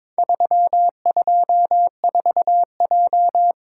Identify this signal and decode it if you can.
Morse code, heard as a keyed tone, about 22 wpm, 690 Hz. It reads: 324J
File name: practice / 22wpm / Four Letter-Number Combinations - 22wpm